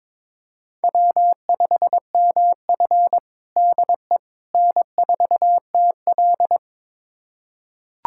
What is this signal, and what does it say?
Morse code, heard as a keyed tone, about 21 wpm, 700 Hz. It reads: W5MF DE N4TL
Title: USA Callsign DE Callsign - 4 Characters - 22wpm